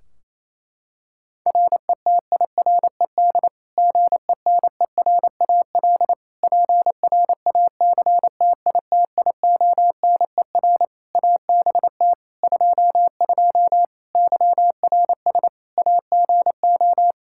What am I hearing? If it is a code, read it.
Morse code, heard as a keyed tone, about 28 wpm, 700 Hz. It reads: RETIRED GENERAL PRACTITIONER ABT 22 YRS AGO
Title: Practice QSO - Exchange 3 - More Info - 28wpm